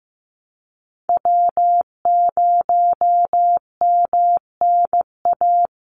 Morse code, heard as a keyed tone, about 15 wpm, 700 Hz. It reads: W0MNA
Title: NA SOTA Chaser Callsigns 15wpm